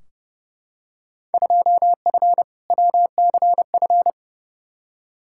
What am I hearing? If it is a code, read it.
Morse code, heard as a keyed tone, about 30 wpm, 700 Hz. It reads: 2F WCF
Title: Field Day Exchanges Encore - 30wpm